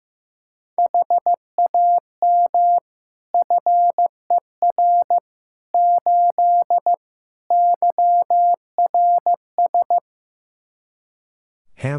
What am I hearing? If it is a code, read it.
Morse code, heard as a keyed tone, about 15 wpm, 700 Hz. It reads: HAM FER 8 YRS
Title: Practice QSO - Exchange 3 - More Info - 15wpm